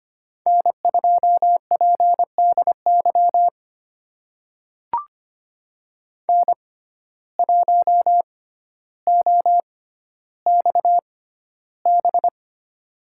Morse code, heard as a keyed tone, about 25 wpm, spaced as for 13 wpm, 700 Hz. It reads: N2PDY N1OXB
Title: USA Call Signs - 5 Characters - 6x Character Spacing - 25wpm